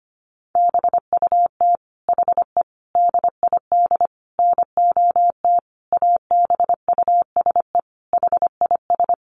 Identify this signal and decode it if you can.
Morse code, heard as a keyed tone, about 25 wpm, 700 Hz. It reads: BUT HE DID NOT ABUSE HIS
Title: Sentences from Top 1300 Words 25wpm